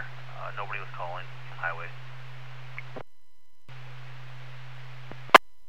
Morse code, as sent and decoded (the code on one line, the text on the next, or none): none